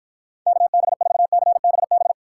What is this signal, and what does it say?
Morse code, heard as a keyed tone, about 54 wpm, 700 Hz. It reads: KB4CBB